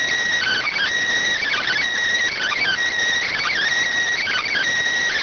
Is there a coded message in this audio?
Yes: Emergency Alert System (SAME) tones (header bursts) from 0.4 to 4.7 s, heard as harsh buzzing bursts.